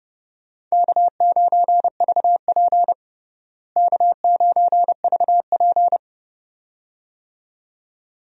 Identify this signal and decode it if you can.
Morse code, heard as a keyed tone, about 30 wpm, 700 Hz. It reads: K9VP K9VP
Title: USA Call Signs - 4 Characters - Repeated 2x - 30wpm